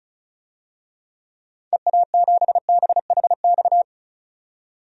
Morse code, heard as a keyed tone, about 34 wpm, 700 Hz. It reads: EA7BHX
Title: DXCC-100 Call Signs - 6 Characters - 35wpm